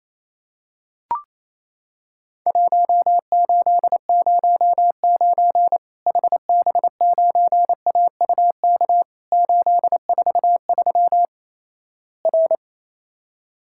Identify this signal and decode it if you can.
Morse code, heard as a keyed tone, about 28 wpm, 700 Hz. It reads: 1809 HB9AUK 843 R
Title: Worked All Europe DX Contest - QTC Exchange - 28wpm